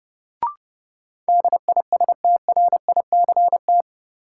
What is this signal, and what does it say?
Morse code, heard as a keyed tone, about 32 wpm, 700 Hz. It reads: DISTRICT
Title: Sentences from Top 1600 Words - New Word Drill Alt - 30wpm